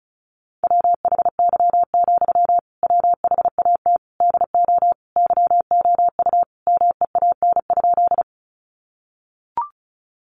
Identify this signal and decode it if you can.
Morse code, heard as a keyed tone, about 34 wpm, 700 Hz. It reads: WHY, WHAT DO YOU MEAN?
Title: Sentences from Top 200 Words 35wpm